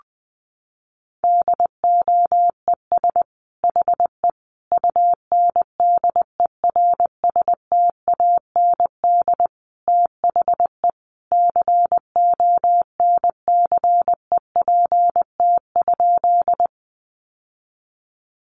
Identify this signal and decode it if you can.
Morse code, heard as a keyed tone, about 20 wpm, 700 Hz. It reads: DOES HE UNDERSTAND THE CONCEPT?
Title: Sentences from Top 1100 Words 20wpm